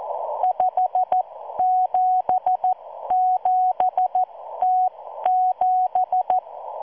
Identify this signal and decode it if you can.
Morse code, heard as a keyed tone, about 14 wpm, spaced as for 12 wpm, 740 Hz. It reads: 577T7